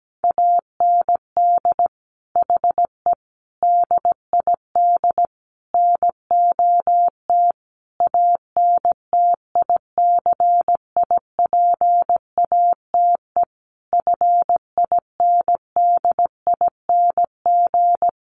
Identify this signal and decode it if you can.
Morse code, heard as a keyed tone, about 17 wpm, 700 Hz. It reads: AND HE DID NOT ANTICIPATE FINDING
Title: Sentences from Top 1500 Words 17wpm